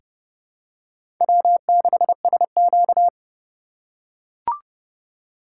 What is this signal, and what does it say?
Morse code, heard as a keyed tone, about 30 wpm, 700 Hz. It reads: W6SQ
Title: USA Call Signs - 4 Characters - 2x Character Spacing - 30wpm